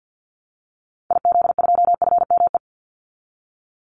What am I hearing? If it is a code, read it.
Morse code, heard as a keyed tone, about 50 wpm, 700 Hz. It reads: IZ2FME
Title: Call Signs Repeated 2x - 50wpm